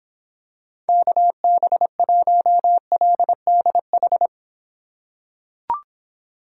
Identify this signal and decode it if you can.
Morse code, heard as a keyed tone, about 26 wpm, 700 Hz. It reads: KB1LDH